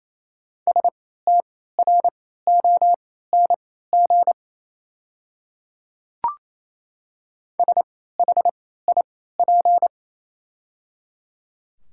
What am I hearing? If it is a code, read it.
Morse code, heard as a keyed tone, about 28 wpm, spaced as for 16 wpm, 690 Hz. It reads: STRONG SHIP
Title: 3x Character Spacing - Top 500 Words - 28wpm